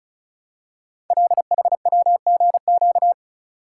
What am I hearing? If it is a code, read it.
Morse code, heard as a keyed tone, about 36 wpm, 700 Hz. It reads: LHWGQ
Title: Code Groups - 5 Letters - 35wpm